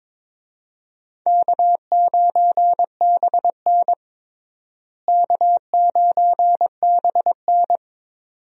Morse code, heard as a keyed tone, about 22 wpm, 700 Hz. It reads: K9BN K9BN
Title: USA Call Signs - 4 Characters - Repeated 2x - 22wpm